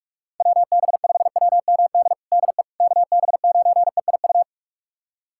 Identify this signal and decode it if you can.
Morse code, heard as a keyed tone, about 46 wpm, 700 Hz. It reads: WB5WMD DE KB9EIU
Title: USA Callsign DE Callsign - 6 Characters - 45wpm